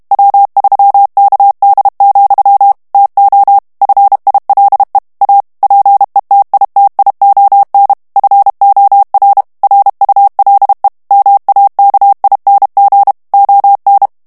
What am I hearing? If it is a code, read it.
Morse code, heard as a keyed tone, about 32 wpm, 800 Hz. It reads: W3KD, TO FILE A PETITION FOR RULE MAKING ON
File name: morse mp3 / ANGLAIS texte clair 3 vitesse 30 mots minute